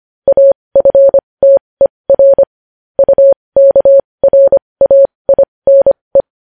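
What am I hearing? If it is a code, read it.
Morse code, heard as a keyed tone, about 25 wpm, spaced as for 20 wpm, 550 Hz. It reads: AFTER UKRAINE
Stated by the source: Morse code news headlines sent at 20.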